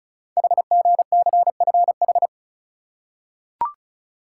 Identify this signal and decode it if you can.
Morse code, heard as a keyed tone, about 36 wpm, 690 Hz. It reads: HGCFH
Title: Code Groups - 5 Letters - 35wpm